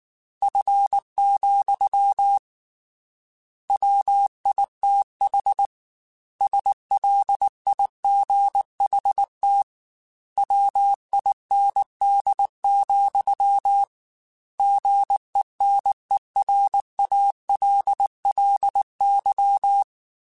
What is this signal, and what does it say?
Morse code, heard as a keyed tone, about 19 wpm, 800 Hz. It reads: F, WITH SLIGHT WIND, GENERALLY
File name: Weather Sets / Weather Sets-mp3s / Weather 4 17 @18